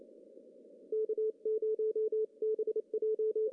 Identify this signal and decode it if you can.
Morse code, heard as a keyed tone, about 28 wpm, 440 Hz. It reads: K0BJ
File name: CW / cqww-cw-k5tr